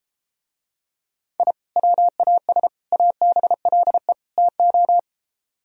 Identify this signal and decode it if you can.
Morse code, heard as a keyed tone, about 33 wpm, 700 Hz. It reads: I WAS ABLE TO